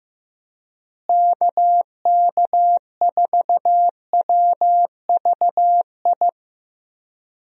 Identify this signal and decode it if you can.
Morse code, heard as a keyed tone, about 14 wpm, 700 Hz. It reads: KK4WVI
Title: USA Call Signs - 6 Characters - Repeated 3x - 15wpm